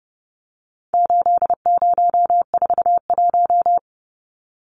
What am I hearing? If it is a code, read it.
Morse code, heard as a keyed tone, about 30 wpm, 700 Hz. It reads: 8041
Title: Numbers - 4-Digit 30wpm